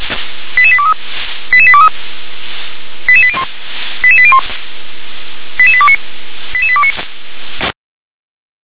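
Five-tone selective call (ZVEI), heard as quick runs of five tones, tones of about 70 ms. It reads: 88823 88823 88801 88801 88829 88829
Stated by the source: ZVEI five-tone selective call 88163 received by SDR.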